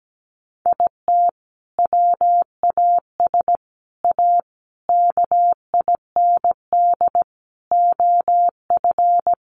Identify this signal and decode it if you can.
Morse code, heard as a keyed tone, about 17 wpm, 700 Hz. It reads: IT WAS A KIND OF
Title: Sentences from Top 500 Words 17wpm